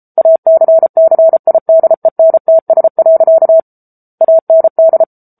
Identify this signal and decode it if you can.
Morse code, heard as a keyed tone, about 34 wpm, 650 Hz. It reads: ACCIDENTS. AND
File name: Book Two / 07-Chapter 7 - 33 WPM